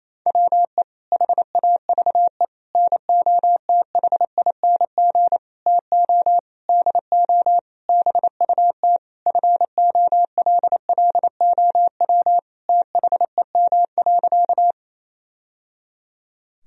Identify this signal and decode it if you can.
Morse code, heard as a keyed tone, about 28 wpm, 700 Hz. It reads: WE HAVE NOTHING TO DO BUT FOLLOW THEM.
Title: Sentences from Top 1000 Words - Review Part 1 - 28wpm